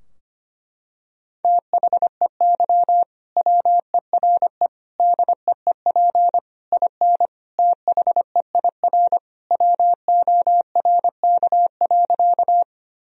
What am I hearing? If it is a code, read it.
Morse code, heard as a keyed tone, about 25 wpm, 700 Hz. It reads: THEY WERE DEEP IN THEIR WORK.
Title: Sentences from Top 1000 Words - Review Part 3 - 25wpm